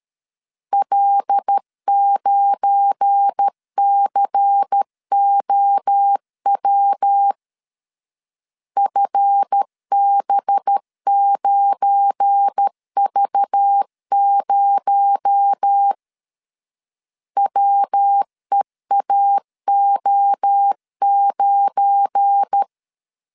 Morse code, heard as a keyed tone, about 13 wpm, 800 Hz. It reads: L9COW FB9V0 WEAO9
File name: LEZIONI / 7 GF9Z / 7 GF9 12wpm 2319